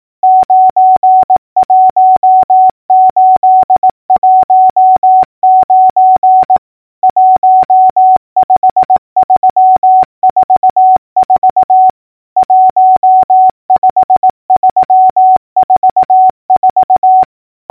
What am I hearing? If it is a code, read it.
Morse code, heard as a keyed tone, about 18 wpm, 750 Hz. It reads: 91819 15344 15344
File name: numbers / V2-morse-clicks